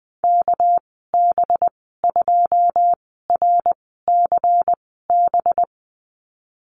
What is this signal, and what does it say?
Morse code, heard as a keyed tone, about 20 wpm, spaced as for 15 wpm, 700 Hz. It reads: KB2RCB
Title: 207 Lesson 33g - Letter j - Call Signs 20wpm